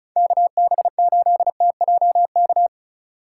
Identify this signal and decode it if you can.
Morse code, heard as a keyed tone, about 34 wpm, 700 Hz. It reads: KB8TJK
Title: USA Call Signs - 6 Characters - 4x Character Spacing - 35wpm